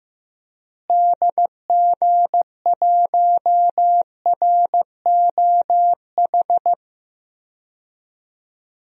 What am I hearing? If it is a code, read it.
Morse code, heard as a keyed tone, about 15 wpm, 700 Hz. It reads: DG1ROH
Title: DXCC-100 Call Signs - 6 Characters - Repeated 2x - 15wpm